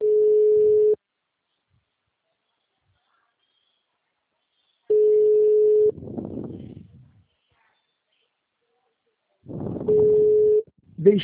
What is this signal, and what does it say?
Ringback tone (Europe, 425 Hz), with speech.
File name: out- -2013- - - .4